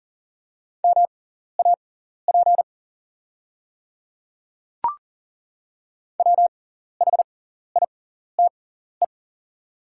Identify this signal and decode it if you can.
Morse code, heard as a keyed tone, about 37 wpm, spaced as for 14 wpm, 690 Hz. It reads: MAP WHITE
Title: 6x Character Spacing - Top 500 Words - 40wpm